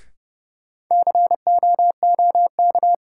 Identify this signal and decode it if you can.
Morse code, heard as a keyed tone, about 31 wpm, 700 Hz. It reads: COOK
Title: Sentences from Top 700 Words - New Word Drill - 30wpm